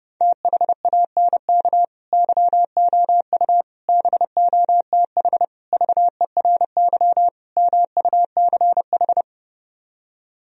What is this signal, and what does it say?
Morse code, heard as a keyed tone, about 32 wpm, 700 Hz. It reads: THANK YOU BOTH VERY MUCH